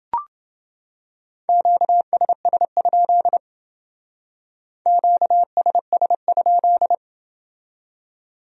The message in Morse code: QSS? QSS?

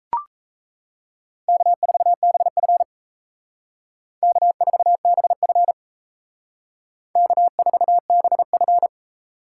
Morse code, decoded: K4BF K4BF K4BF

